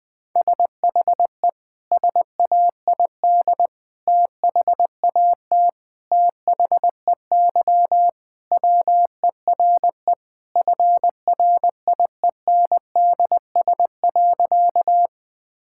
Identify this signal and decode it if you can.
Morse code, heard as a keyed tone, about 20 wpm, 700 Hz. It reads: SHE SAID THAT THEY WERE FRIENDS.
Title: Sentences with 3-Word Phrases - Spoken Beginning - 20wpm